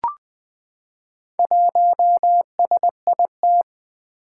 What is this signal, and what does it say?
Morse code, heard as a keyed tone, about 20 wpm, 700 Hz. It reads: 1SIT
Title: Four Challenging Letter-Number Combinations 20wpm